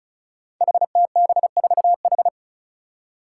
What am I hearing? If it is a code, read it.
Morse code, heard as a keyed tone, about 36 wpm, 700 Hz. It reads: HTB4H
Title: Five Challenging Letter-Number Combinations 35wpm